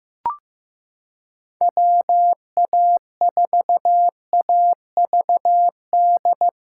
Morse code, heard as a keyed tone, about 15 wpm, 700 Hz. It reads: WA4AVD
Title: USA Callsign DE Callsign 15wpm